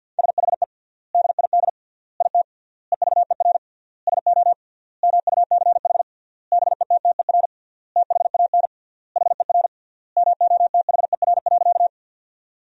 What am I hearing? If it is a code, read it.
Morse code, heard as a keyed tone, about 50 wpm, 700 Hz. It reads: SHE DID IT EVER SO MUCH BETTER THAN HER MOTHER.
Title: 3x Word-Spacing - Sentences from Top 400 Words - 50wpm